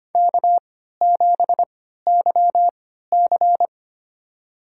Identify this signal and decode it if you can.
Morse code, heard as a keyed tone, about 25 wpm, spaced as for 14 wpm, 700 Hz. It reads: K7YC
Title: USA Call Signs - 3x Character Spacing - 25wpm